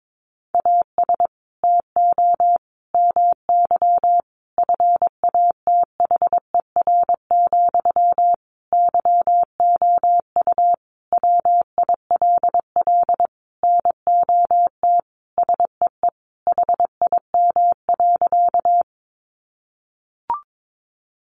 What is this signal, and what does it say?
Morse code, heard as a keyed tone, about 22 wpm, 700 Hz. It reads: AS TO MY FATHER, YOU WILL NOT SEE HIM.
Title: Sentences from Top 600 Words 22wpm